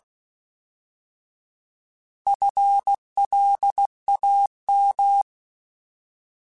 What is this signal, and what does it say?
Morse code, heard as a keyed tone, about 16 wpm, 800 Hz. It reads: FLAM
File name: Alphabetical / Alphabetical mp3s / Fs @15